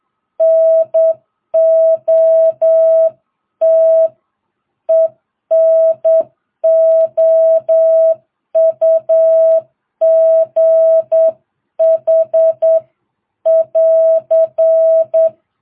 Morse code, heard as a keyed tone, about 8 wpm, 660 Hz. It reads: NOT ENOUGH <AR>